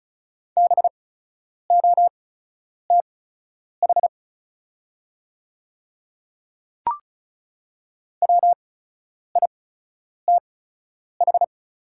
Morse code, heard as a keyed tone, about 36 wpm, spaced as for 10 wpm, 700 Hz. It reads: BOTH WITH
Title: 8x Character Spacing - Top 300 Words - 35wpm